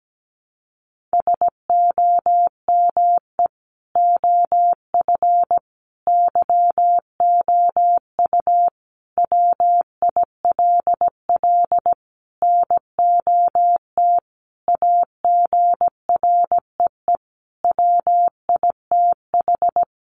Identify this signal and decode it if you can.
Morse code, heard as a keyed tone, about 17 wpm, 700 Hz. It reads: SOME OF YOU WILL NOT AGREE WITH
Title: Sentences from Top 1100 Words 17wpm